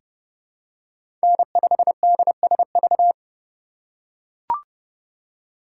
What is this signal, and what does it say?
Morse code, heard as a keyed tone, about 30 wpm, 700 Hz. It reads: N5DSV
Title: USA Call Signs - 5 Characters - 2x Character Spacing - 30wpm